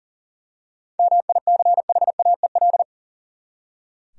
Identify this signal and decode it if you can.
Morse code, heard as a keyed tone, about 40 wpm, 700 Hz. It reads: MICHAEL
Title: USA Common Names - 40wpm